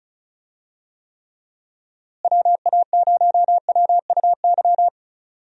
Morse code, heard as a keyed tone, about 34 wpm, 700 Hz. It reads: WA0WUY
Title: USA Call Signs - 6 Characters - Repeated 3x - 35wpm